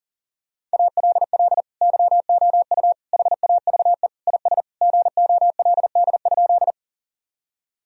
Morse code, heard as a keyed tone, about 40 wpm, 700 Hz. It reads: ALL YOU HAVE IS GOLD?
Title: Sentences from Top 1000 Words - Review Part 3 - 40wpm